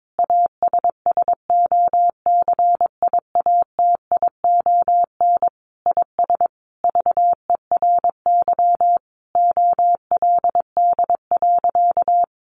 Morse code, heard as a keyed tone, about 22 wpm, 700 Hz. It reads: ASSOCIATION IS VERY OLD.
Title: Sentences from Top 1100 Words 22wpm